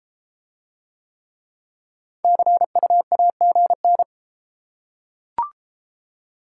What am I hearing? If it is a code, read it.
Morse code, heard as a keyed tone, about 31 wpm, 700 Hz. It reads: CUAGN